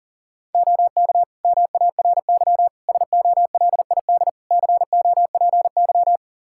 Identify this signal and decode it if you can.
Morse code, heard as a keyed tone, about 40 wpm, 700 Hz. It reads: OK MARY SOLID COPY
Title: Practice QSO - Exchange 3 - More Info - 40wpm